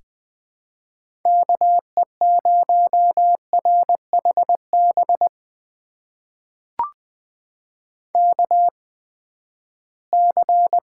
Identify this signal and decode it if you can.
Morse code, heard as a keyed tone, about 21 wpm, spaced as for 14 wpm, 700 Hz. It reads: KE0RHB KC
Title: USA Call Signs - 6 Characters - 8x Character Spacing - 20wpm